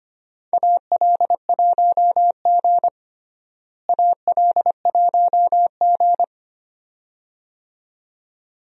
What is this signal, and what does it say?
Morse code, heard as a keyed tone, about 25 wpm, 700 Hz. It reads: AL1G AL1G
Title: USA Call Signs - 4 Characters - Repeated 2x - 25wpm